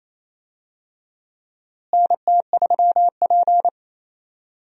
Morse code, heard as a keyed tone, about 28 wpm, 700 Hz. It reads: NT3P
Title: USA Call Signs - 4 Characters - Repeated 3x - 28wpm